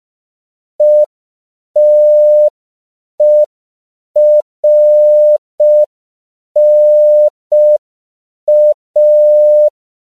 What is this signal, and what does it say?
Morse code, heard as a keyed tone, about 5 wpm, 600 Hz. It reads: ETERNA
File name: CW / 5PPM